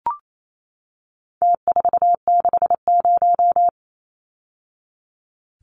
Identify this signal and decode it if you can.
Morse code, heard as a keyed tone, about 28 wpm, 700 Hz. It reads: T460
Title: Numbers - Contest Style - 4-Digit 28wpm